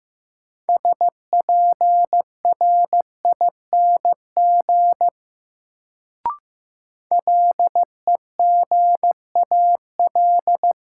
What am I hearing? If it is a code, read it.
Morse code, heard as a keyed tone, about 15 wpm, 700 Hz. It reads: SPRING LEGAL
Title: Sentences from Top 1200 Words - New Word Drill Alt - 15wpm